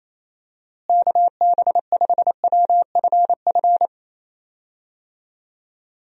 Morse code, heard as a keyed tone, about 28 wpm, 700 Hz. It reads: KB5WFF
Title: USA Call Signs - 6 Characters - Repeated 3x - 28wpm